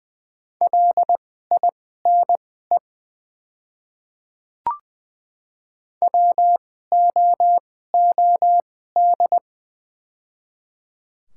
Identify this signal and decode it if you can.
Morse code, heard as a keyed tone, about 20 wpm, spaced as for 14 wpm, 700 Hz. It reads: LINE WOOD
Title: 2x Character Spacing - Top 300 Words - 20wpm